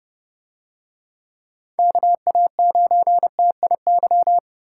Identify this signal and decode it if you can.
Morse code, heard as a keyed tone, about 29 wpm, 700 Hz. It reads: KA9TIY